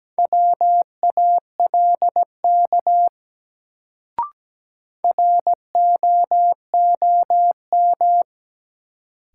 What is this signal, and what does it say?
Morse code, heard as a keyed tone, about 17 wpm, 700 Hz. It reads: WALK ROOM
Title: Sentences from Top 300 Words - New Word Drill - 17wpm